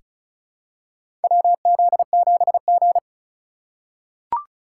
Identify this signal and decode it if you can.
Morse code, heard as a keyed tone, about 36 wpm, 700 Hz. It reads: WZ7G
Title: USA Call Signs - 4 Characters - 4x Character Spacing - 35wpm